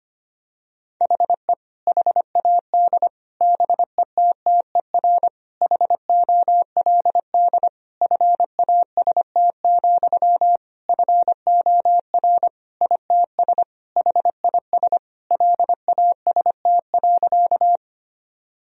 Morse code, heard as a keyed tone, about 25 wpm, 700 Hz. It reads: HE HAD BETTER HOLD FAST, FOR ITS HIS LAST.
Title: Sentences from Top 1000 Words - Review Part 2 - 25wpm